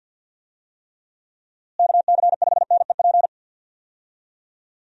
Morse code, heard as a keyed tone, about 51 wpm, 700 Hz. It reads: KC5NEP
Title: USA Call Signs - 6 Characters - Repeated 2x - 50wpm